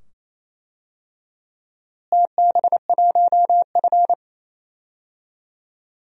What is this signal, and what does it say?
Morse code, heard as a keyed tone, about 27 wpm, 700 Hz. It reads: TB1F
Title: DXCC-100 Call Signs - 4 Characters - Repeated 2x - 28wpm